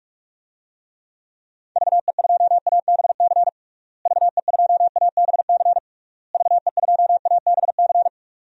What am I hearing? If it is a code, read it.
Morse code, heard as a keyed tone, about 41 wpm, 700 Hz. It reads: VE2ABC VE2ABC VE2ABC
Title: DXCC-100 Call Signs - 6 Characters - Repeated 3x - 45wpm